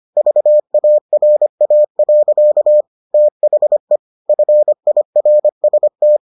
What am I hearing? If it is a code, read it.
Morse code, heard as a keyed tone, about 25 wpm, 600 Hz. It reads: VARA. THE FIRST